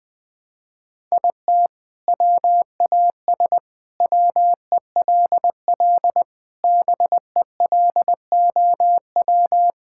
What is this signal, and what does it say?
Morse code, heard as a keyed tone, about 21 wpm, 700 Hz. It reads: IT WAS WELL BELOW
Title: Sentences from Top 1000 Words - Review Part 6 - 20wpm